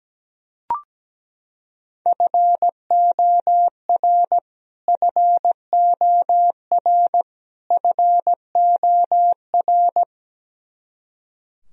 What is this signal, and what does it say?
Morse code, heard as a keyed tone, about 17 wpm, 700 Hz. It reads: FOR FOR FOR
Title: Top 100 Words Repeated 17wpm